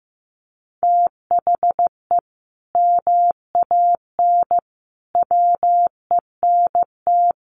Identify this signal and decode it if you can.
Morse code, heard as a keyed tone, about 15 wpm, 700 Hz. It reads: THE MAN WENT
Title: Sentences from Top 1400 Words 15wpm